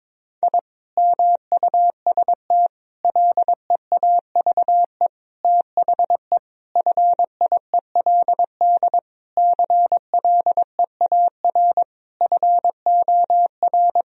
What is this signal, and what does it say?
Morse code, heard as a keyed tone, about 22 wpm, 700 Hz. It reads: I MUST LEAVE THE FIELD CLEAR FOR
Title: Sentences from Top 1000 Words - Review Part 2 - 22wpm